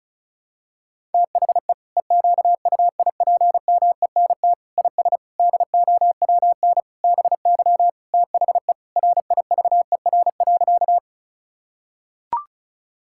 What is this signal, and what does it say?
Morse code, heard as a keyed tone, about 36 wpm, 700 Hz. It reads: THE EQUIPMENT IS DOWN BY THE RIVER.
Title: Sentences from Top 1000 Words - Review Part 5 - 35wpm